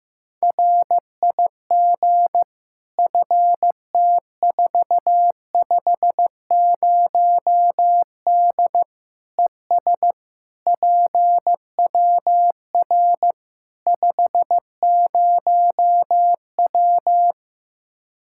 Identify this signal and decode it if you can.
Morse code, heard as a keyed tone, about 15 wpm, 700 Hz. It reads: RIG FT450D ES PWR 50W
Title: K1USN SST QSO Practice 15wpm